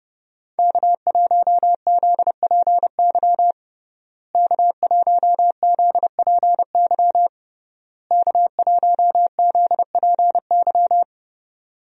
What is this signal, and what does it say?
Morse code, heard as a keyed tone, about 30 wpm, 700 Hz. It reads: K1ZPY K1ZPY K1ZPY
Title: USA Call Signs - 5 Characters - Repeated 3x - 30wpm